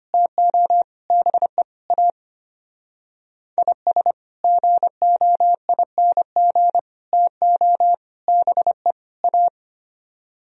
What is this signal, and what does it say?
Morse code, heard as a keyed tone, about 31 wpm, 700 Hz. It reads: TO BE A IS GOING TO BE A